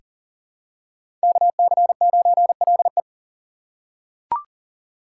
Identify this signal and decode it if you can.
Morse code, heard as a keyed tone, about 40 wpm, 700 Hz. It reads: KC9LE
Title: USA Call Signs - 6x Character Spacing - 40wpm